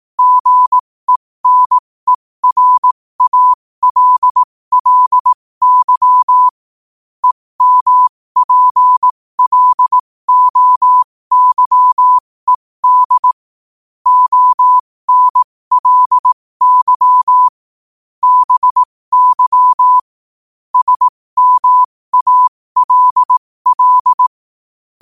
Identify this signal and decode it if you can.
Morse code, heard as a keyed tone, about 18 wpm, spaced as for 15 wpm, 1000 Hz. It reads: GENERALLY EMPLOYED ONLY BY SMALL
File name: qotdfiles / 15wpm